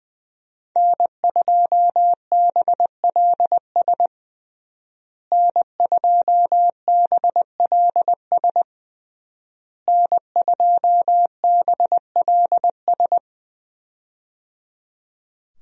Morse code, heard as a keyed tone, about 21 wpm, 700 Hz. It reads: N2BLS N2BLS N2BLS
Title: USA Call Signs - 5 Characters - Repeated 3x - 20wpm